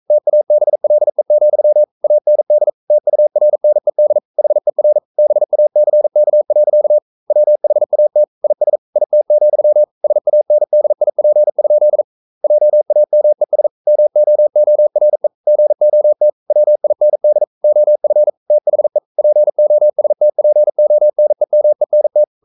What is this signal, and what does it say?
Morse code, heard as a keyed tone, about 42 wpm, 600 Hz. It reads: TABLE, AND TURNED HER BACK. WHAT IS IT, SANDIP? JAMES MOORE GOT WIND OF THE POSTPONEMENT
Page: Romanen Never af Ken Follet på engelsk.